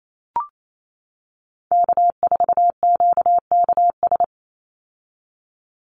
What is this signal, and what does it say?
Morse code, heard as a keyed tone, about 28 wpm, 700 Hz. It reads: K4QKS
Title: USA Call Signs 28wpm